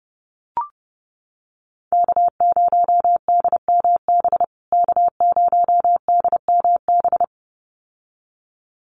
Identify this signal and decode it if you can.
Morse code, heard as a keyed tone, about 30 wpm, 700 Hz. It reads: K0DMB K0DMB
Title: USA Call Signs Repeated 2x 30wpm